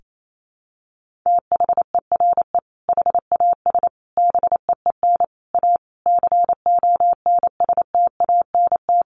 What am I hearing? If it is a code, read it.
Morse code, heard as a keyed tone, about 28 wpm, 700 Hz. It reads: THERE HAS BEEN A CONSTANT
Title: Sentences from Top 1500 Words 28wpm